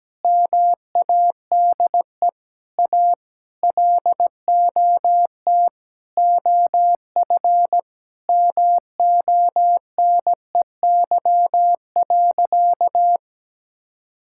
Morse code, heard as a keyed tone, about 17 wpm, 700 Hz. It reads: MADE A LOT OF MONEY.
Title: Sentences from Top 1000 Words - Review Part 6 - 17wpm